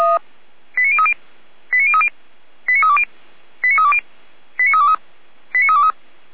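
Five-tone selective call (ZVEI), heard as quick runs of five tones, tones of about 70 ms. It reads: 89030 89030 89320 89320 89323 89323
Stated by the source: ZVEI five-tone selective call received by SDR, from the feuerwehr-orlamuende.org fire brigade archive.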